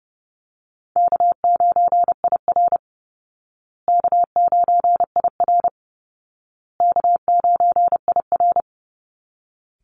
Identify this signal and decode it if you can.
Morse code, heard as a keyed tone, about 30 wpm, 700 Hz. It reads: K9IR K9IR K9IR
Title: Easy Call Signs Repeated 3x - 30wpm